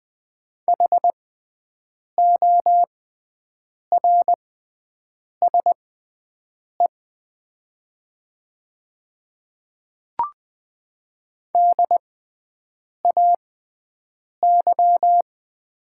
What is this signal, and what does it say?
Morse code, heard as a keyed tone, about 20 wpm, spaced as for 7 wpm, 700 Hz. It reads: HORSE DAY
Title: 6x Character Spacing - Top 500 Words - 20wpm